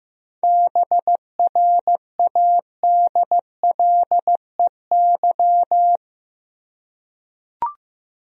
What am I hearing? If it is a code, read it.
Morse code, heard as a keyed tone, about 16 wpm, 700 Hz. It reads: BRADLEY
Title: USA Common Names - 15wpm